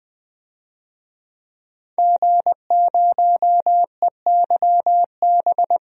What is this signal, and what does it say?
Morse code, heard as a keyed tone, about 21 wpm, 700 Hz. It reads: G0EYB
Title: DXCC-100 Call Signs - 5 Characters - 20wpm